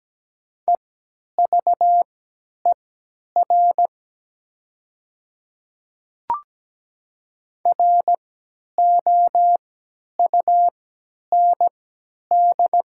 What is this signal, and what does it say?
Morse code, heard as a keyed tone, about 17 wpm, spaced as for 10 wpm, 700 Hz. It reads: EVER ROUND
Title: 3x Character Spacing - Top 500 Words - 17wpm